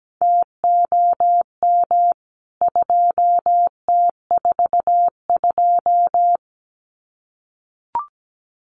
Morse code, heard as a keyed tone, about 17 wpm, 700 Hz. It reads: TOM 2T42
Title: CWT Exchange 17wpm